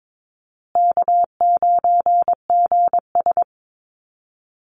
Morse code, heard as a keyed tone, about 22 wpm, 700 Hz. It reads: K9GS
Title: Easy Call Signs Repeated 3x - 22wpm